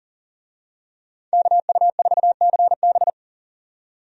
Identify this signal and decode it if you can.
Morse code, heard as a keyed tone, about 40 wpm, 700 Hz. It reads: KU4CB
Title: USA Call Signs - 5 Characters - 40wpm